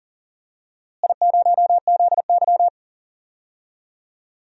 Morse code, heard as a keyed tone, about 40 wpm, 700 Hz. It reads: I0ZY
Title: DXCC-100 Call Signs - 4 Characters - Repeated 3x - 40wpm